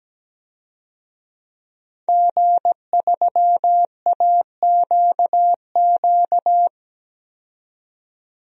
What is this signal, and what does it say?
Morse code, heard as a keyed tone, about 17 wpm, 700 Hz. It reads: G3AQQ
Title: DXCC-100 Call Signs - 5 Characters - 17wpm